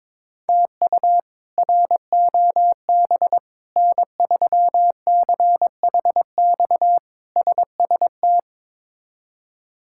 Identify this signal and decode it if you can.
Morse code, heard as a keyed tone, about 22 wpm, 700 Hz. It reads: TU ROB N3CHX SST
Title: K1USN SST QSO Practice - 22wpm